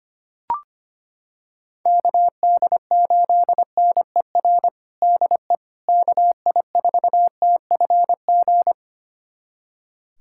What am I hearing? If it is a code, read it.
Morse code, heard as a keyed tone, about 25 wpm, 700 Hz. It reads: KD8NER DE KI4TFG
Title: USA Callsign DE Callsign - 6 Characters - 25wpm